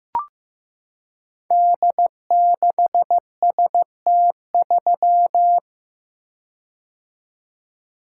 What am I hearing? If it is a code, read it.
Morse code, heard as a keyed tone, about 15 wpm, 700 Hz. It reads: D6ST3
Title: Code Groups - 5 Letter:Numbers - 15wpm